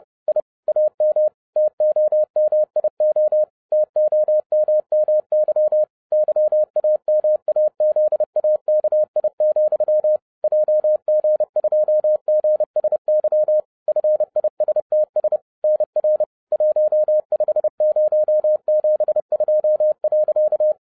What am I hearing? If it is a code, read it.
Morse code, heard as a keyed tone, about 30 wpm, 610 Hz. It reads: I AM TOMIO TOMMY YAMAZAKI, JG2GSY FISTS NR 15072.
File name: 2011-07-mp3 / n40 EXCITING QRP CW OPERATION 30wpm